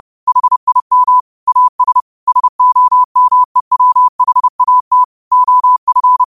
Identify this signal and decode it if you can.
Morse code, heard as a keyed tone, about 30 wpm, 1000 Hz. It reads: HIM AS SOMEWHAT OF